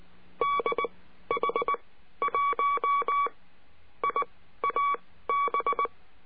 Morse code, heard as a keyed tone, about 19 wpm, spaced as for 12 wpm, 1130 Hz. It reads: DH1IAB
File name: CURSO DE TELEGRAFIA